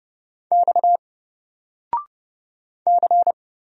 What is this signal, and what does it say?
Morse code, heard as a keyed tone, about 30 wpm, 700 Hz. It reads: X C
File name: Single-Letter-Number-30wpm